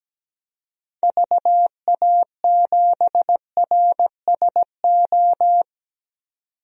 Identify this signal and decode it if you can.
Morse code, heard as a keyed tone, about 17 wpm, 700 Hz. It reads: VA7RSO
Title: DXCC-100 Call Signs - 6 Characters - 17wpm